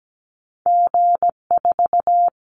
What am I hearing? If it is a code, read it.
Morse code, heard as a keyed tone, about 17 wpm, 700 Hz. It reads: G4
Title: Callsign Prefixes for Top 25 DXCC - Repeated 3x - 17wpm